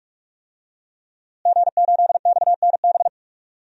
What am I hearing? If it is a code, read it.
Morse code, heard as a keyed tone, about 42 wpm, 700 Hz. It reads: G8XNB
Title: DXCC-100 Call Signs - 5 Characters - Repeated 2x - 45wpm